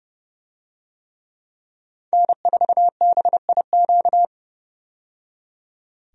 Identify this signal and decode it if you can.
Morse code, heard as a keyed tone, about 30 wpm, 700 Hz. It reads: N4BIQ
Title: USA Call Signs - 5 Characters - 30wpm